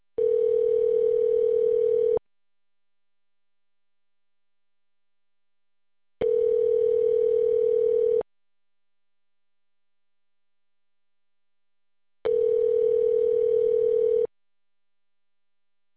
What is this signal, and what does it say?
Ringback tone (North America), 440 + 480 Hz.